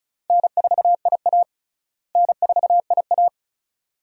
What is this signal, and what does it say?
Morse code, heard as a keyed tone, about 36 wpm, 700 Hz. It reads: N4IA N4IA
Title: USA Call Signs - 4 Characters - Repeated 2x - 35wpm